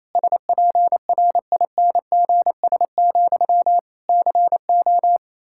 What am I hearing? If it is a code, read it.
Morse code, heard as a keyed tone, about 28 wpm, 700 Hz. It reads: SPRINGS, CO
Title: Practice QSO - Exchange 1 - Beginning - 28wpm